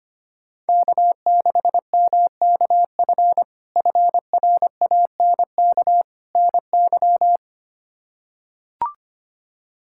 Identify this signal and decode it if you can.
Morse code, heard as a keyed tone, about 25 wpm, 700 Hz. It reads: K6MKF FRANK NY
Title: K1USN SST QSO Practice - 25wpm